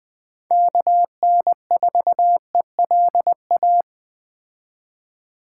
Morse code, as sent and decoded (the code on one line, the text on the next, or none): -.- -. ....- . .-.. .-
KN4ELA